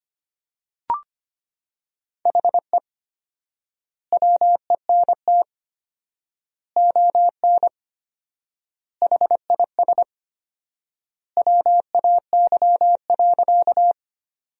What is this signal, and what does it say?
Morse code, heard as a keyed tone, about 25 wpm, 700 Hz. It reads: HE WENT ON HIS WAY.
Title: 4x Word-Spacing - Sentences from Top 200 Words - 25wpm